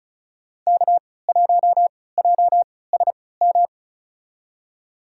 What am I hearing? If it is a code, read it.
Morse code, heard as a keyed tone, about 36 wpm, spaced as for 20 wpm, 700 Hz. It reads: K1JSM
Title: USA Call Signs - 3x Character Spacing - 35wpm